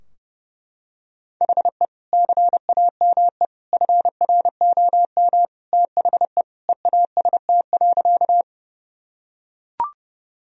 Morse code, heard as a keyed tone, about 30 wpm, 700 Hz. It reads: HE CAME FROM THE EAST.
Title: Sentences from Top 1000 Words - Review Part 3 - 30wpm